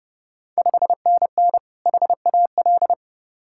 Morse code, heard as a keyed tone, about 30 wpm, 690 Hz. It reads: 5NN HAL
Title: Ontario QSO Party Exchange 30wpm